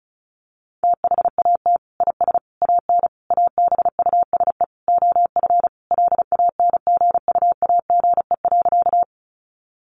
Morse code, heard as a keyed tone, about 34 wpm, 700 Hz. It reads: THAT IS AN ABUSE OF LANGUAGE.
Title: Sentences from Top 1300 Words 35wpm